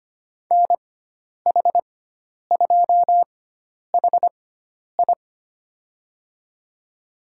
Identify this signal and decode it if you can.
Morse code, heard as a keyed tone, about 25 wpm, spaced as for 10 wpm, 700 Hz. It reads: NH2HI
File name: Lesson 16g - Letter c - Call Signs 25wpm